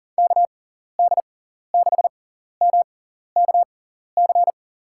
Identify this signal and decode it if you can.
Morse code, heard as a keyed tone, about 40 wpm, spaced as for 14 wpm, 700 Hz. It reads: KD6MKC
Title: USA Call Signs - 6 Characters - 6x Character Spacing - 40wpm